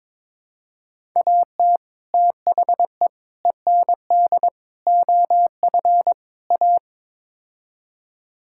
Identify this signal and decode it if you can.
Morse code, heard as a keyed tone, about 21 wpm, 700 Hz. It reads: AT THE END OF A
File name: practice / 22wpm / Build A Phrase - 5-Word Phrases - 22wpm